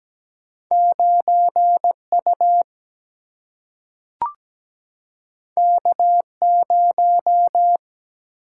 Morse code, heard as a keyed tone, about 17 wpm, 700 Hz. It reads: 9U K0
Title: Single-Letter Single-Number 17wpm